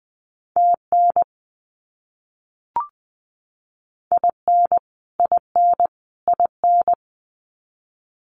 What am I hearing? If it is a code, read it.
Morse code, heard as a keyed tone, about 20 wpm, 700 Hz. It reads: TN IN IN IN
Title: US States Repeated 3x 20wpm